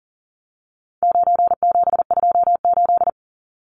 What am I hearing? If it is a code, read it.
Morse code, heard as a keyed tone, about 40 wpm, 700 Hz. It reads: 9728
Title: Numbers - 4-Digit 40wpm